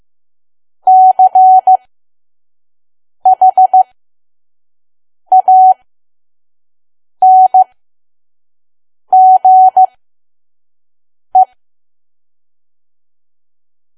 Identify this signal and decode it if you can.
Morse code, heard as a keyed tone, about 15 wpm, spaced as for 5 wpm, 750 Hz. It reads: CHANGE
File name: Archive / 5 WPM files / PM